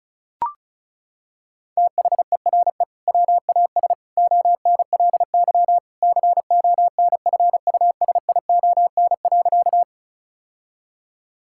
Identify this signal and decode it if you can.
Morse code, heard as a keyed tone, about 34 wpm, 700 Hz. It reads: THERE WAS ONLY CONFUSION.
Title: Sentences from Top 1600 Words 35wpm